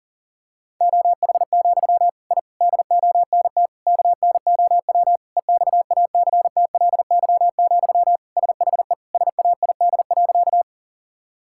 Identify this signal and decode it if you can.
Morse code, heard as a keyed tone, about 37 wpm, 700 Hz. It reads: OH, I DONT KNOW EXACTLY, SHE SAID.
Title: Sentences from Top 1000 Words - Review Part 6 - 40wpm